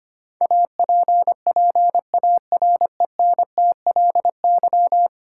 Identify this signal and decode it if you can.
Morse code, heard as a keyed tone, about 25 wpm, 700 Hz. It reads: APPARENTLY
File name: Ten Letter Words - 25wpm